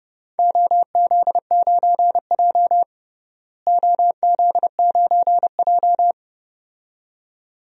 Morse code, heard as a keyed tone, about 30 wpm, 700 Hz. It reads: OZ9J OZ9J
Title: DXCC-100 Call Signs - 4 Characters - Repeated 2x - 30wpm